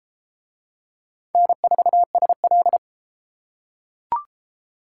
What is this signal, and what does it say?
Morse code, heard as a keyed tone, about 32 wpm, 700 Hz. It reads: N4SL